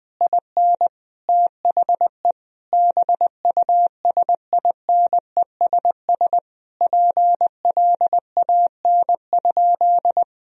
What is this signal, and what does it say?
Morse code, heard as a keyed tone, about 20 wpm, 700 Hz. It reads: IN THE BUSINESS PLAN?
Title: Sentences from Top 1000 Words - Review Part 3 - 20wpm